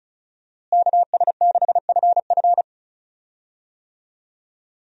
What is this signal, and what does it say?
Morse code, heard as a keyed tone, about 36 wpm, 700 Hz. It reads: KS6FF
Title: USA Call Signs - 5 Characters - Repeated 2x - 35wpm